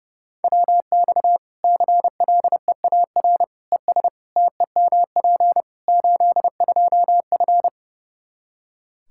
Morse code, heard as a keyed tone, about 30 wpm, 700 Hz. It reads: WX CLEAR ES TEMP 82F
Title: Practice QSO - Exchange 2 - Info - 30wpm